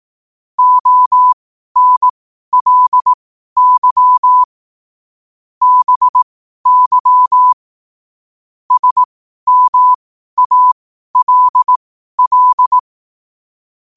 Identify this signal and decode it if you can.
Morse code, heard as a keyed tone, about 18 wpm, spaced as for 12 wpm, 1000 Hz. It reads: ONLY BY SMALL
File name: 12wpm